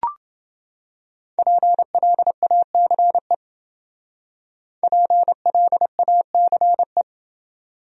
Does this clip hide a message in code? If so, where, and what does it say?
Yes: Morse code from 1.4 to 7.0 s, heard as a keyed tone, about 28 wpm, 700 Hz. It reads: PLACE PLACE